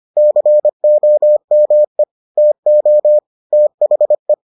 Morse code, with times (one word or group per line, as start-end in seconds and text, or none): 0.2-2.0 COME
2.4-3.2 TO
3.5-4.3 THE